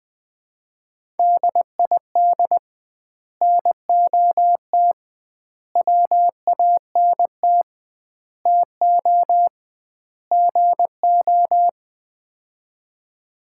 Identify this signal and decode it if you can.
Morse code, heard as a keyed tone, about 20 wpm, 700 Hz. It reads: DID NOT WANT TO GO
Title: 2x Word-Spacing - Sets of 5 Words - 20wpm